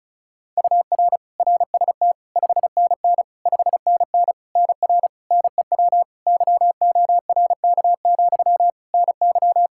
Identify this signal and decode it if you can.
Morse code, heard as a keyed tone, about 34 wpm, 700 Hz. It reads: UR RST 5NN 5NN NR NEW YORK, NY
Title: Practice QSO - Exchange 1 - Beginning - 35wpm